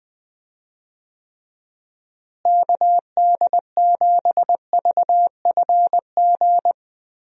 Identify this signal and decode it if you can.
Morse code, heard as a keyed tone, about 20 wpm, 700 Hz. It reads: KD7VFG
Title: USA Call Signs - 6 Characters - Repeated 2x - 20wpm